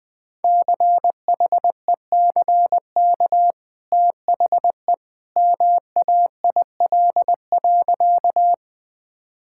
Morse code, heard as a keyed tone, about 20 wpm, 700 Hz. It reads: CHECK THE MAIL.
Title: Easy Sentences with Hints - 20wpm